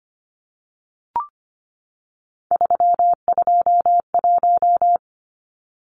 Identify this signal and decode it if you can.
Morse code, heard as a keyed tone, about 25 wpm, 700 Hz. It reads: 321
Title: Numbers - 3-Digit 25wpm